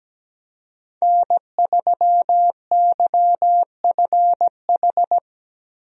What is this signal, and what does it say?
Morse code, heard as a keyed tone, about 17 wpm, 700 Hz. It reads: N3YFH
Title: USA Call Signs - 5 Characters - 17wpm